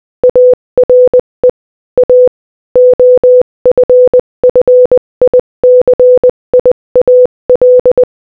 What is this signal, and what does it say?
Morse code, heard as a keyed tone, about 20 wpm, 500 Hz. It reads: ARE A OFFICIAL
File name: Morse-Code-Audio-Speed-20-10